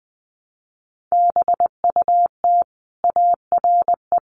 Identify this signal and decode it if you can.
Morse code, heard as a keyed tone, about 20 wpm, 700 Hz. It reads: BUT ARE
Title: Sentences from Top 300 Words 20wpm